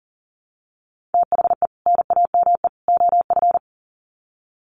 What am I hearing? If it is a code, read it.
Morse code, heard as a keyed tone, about 38 wpm, 700 Hz. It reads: THE NAME OF
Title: Sets of 3 Words 40wpm